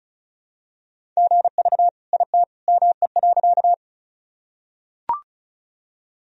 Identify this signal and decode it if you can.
Morse code, heard as a keyed tone, about 36 wpm, 700 Hz. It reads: GV IT ME.
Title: Sentences from Top 200 Words and Abbreviations 35wpm